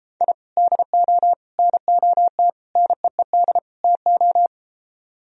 Morse code, heard as a keyed tone, about 33 wpm, 700 Hz. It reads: I DO NOT NEED TO